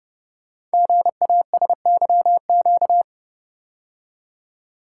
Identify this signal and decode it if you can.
Morse code, heard as a keyed tone, about 30 wpm, 700 Hz. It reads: GASYQ
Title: Code Groups - 5 Letter:Numbers - 30wpm